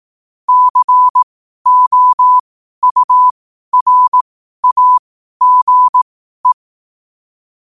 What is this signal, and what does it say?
Morse code, heard as a keyed tone, about 18 wpm, spaced as for 12 wpm, 1000 Hz. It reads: COURAGE
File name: qotdfiles / 12wpm